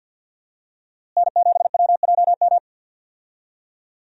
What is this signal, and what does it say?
Morse code, heard as a keyed tone, about 47 wpm, 700 Hz. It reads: N7WJM
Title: USA Call Signs - 5 Characters - Repeated 2x - 50wpm